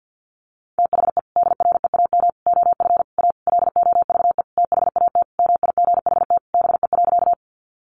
Morse code, heard as a keyed tone, about 50 wpm, 700 Hz. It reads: THE DREAM OF A LOVE THAT MIGHT BE.
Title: Sentences from Top 1500 Words 50wpm